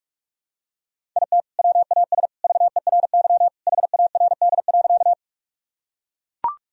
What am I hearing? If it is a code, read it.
Morse code, heard as a keyed tone, about 41 wpm, 700 Hz. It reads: IT WAS VERY HARD.
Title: Sentences from Top 1000 Words - Review Part 1 - 45wpm